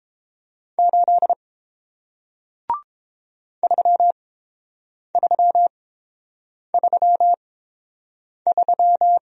Morse code, heard as a keyed tone, about 30 wpm, spaced as for 7 wpm, 700 Hz. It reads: 8 3333